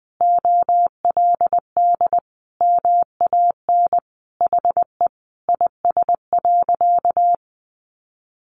Morse code, heard as a keyed tone, about 20 wpm, 700 Hz. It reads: OLD MAN HE IS.
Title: Sentences from Top 1100 Words 20wpm